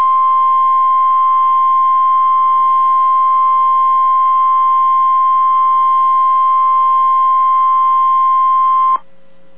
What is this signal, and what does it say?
Emergency Alert System (SAME) tones (the 1050 Hz attention tone). It is heard from 0.2 to 9.0 s.